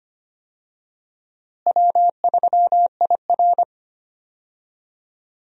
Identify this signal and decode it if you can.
Morse code, heard as a keyed tone, about 25 wpm, 700 Hz. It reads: W3IR
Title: USA Call Signs - 4 Characters - Repeated 3x - 25wpm